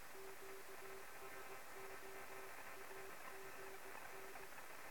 Morse code, heard as a keyed tone, about 18 wpm, 390 Hz, with noise. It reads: QSO AGN
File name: sm7lcb / 2 oh2pu-p 23cm